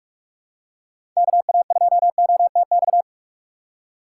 Morse code, heard as a keyed tone, about 46 wpm, 700 Hz. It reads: KA2OTX
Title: USA Call Signs - 6 Characters - 2x Character Spacing - 45wpm